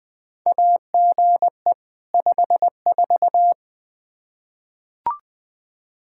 Morse code, heard as a keyed tone, about 20 wpm, 700 Hz. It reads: AGE 54
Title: Practice QSO - Exchange 3 - More Info - 20wpm